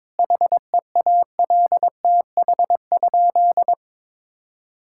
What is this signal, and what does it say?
Morse code, heard as a keyed tone, about 21 wpm, 700 Hz. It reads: HEALTH?
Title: Sentences from Top 1000 Words - Review Part 5 - 22wpm